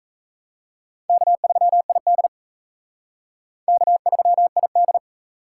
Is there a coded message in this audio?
Yes: Morse code from 1.1 to 5.0 s, heard as a keyed tone, about 35 wpm, 700 Hz. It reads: K3ID K3ID